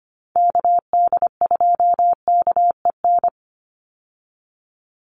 Morse code, heard as a keyed tone, about 25 wpm, 700 Hz. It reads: KD2KEN
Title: USA Call Signs 25wpm